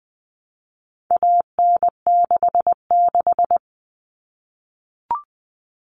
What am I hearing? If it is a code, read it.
Morse code, heard as a keyed tone, about 20 wpm, 700 Hz. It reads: AN66
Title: All Cut Numbers - 4-Digit 20wpm